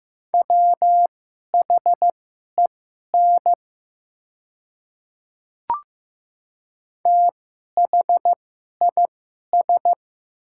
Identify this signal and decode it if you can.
Morse code, heard as a keyed tone, about 15 wpm, spaced as for 11 wpm, 690 Hz. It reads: WHEN THIS
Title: 2x Character Spacing - Top 100 Words - 15wpm